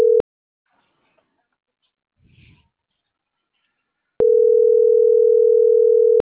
Ringback tone (North America), 440 + 480 Hz.